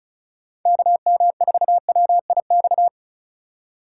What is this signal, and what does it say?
Morse code, heard as a keyed tone, about 36 wpm, 700 Hz. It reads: KM4WIX
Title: USA Call Signs - 8x Character Spacing - 35wpm